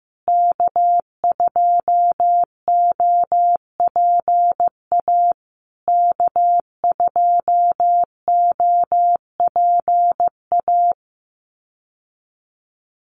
Morse code, heard as a keyed tone, about 15 wpm, 700 Hz. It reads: K2OPA K2OPA
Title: USA Call Signs Repeated 3x 15wpm